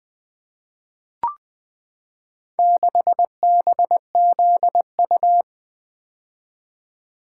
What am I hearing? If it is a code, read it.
Morse code, heard as a keyed tone, about 20 wpm, 700 Hz. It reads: 6BZU
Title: Four Letter-Number Combinations - 20wpm